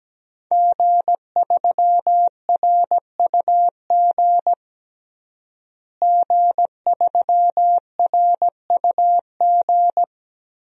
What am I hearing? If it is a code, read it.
Morse code, heard as a keyed tone, about 17 wpm, 700 Hz. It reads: G3RUG G3RUG
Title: DXCC-100 Call Signs - 5 Characters - Repeated 2x - 17wpm